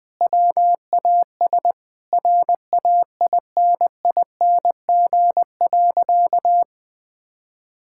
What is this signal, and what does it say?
Morse code, heard as a keyed tone, about 20 wpm, 700 Hz. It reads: WAS RAINING.
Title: Sentences with 3-Word Phrases - 20wpm